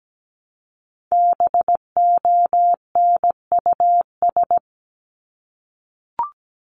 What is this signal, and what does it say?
Morse code, heard as a keyed tone, about 17 wpm, 700 Hz. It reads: BONUS
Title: Top 1751-2000 Words 17wpm